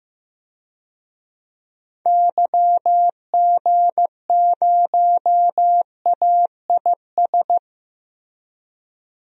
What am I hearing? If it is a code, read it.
Morse code, heard as a keyed tone, about 15 wpm, 700 Hz. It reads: YG0AIS
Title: DXCC-100 Call Signs - 6 Characters - Repeated 3x - 15wpm